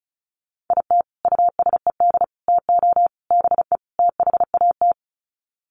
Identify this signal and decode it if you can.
Morse code, heard as a keyed tone, about 34 wpm, 700 Hz. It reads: IT USED TO BE THAT
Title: Sets of 5 Words Encore2 35wpm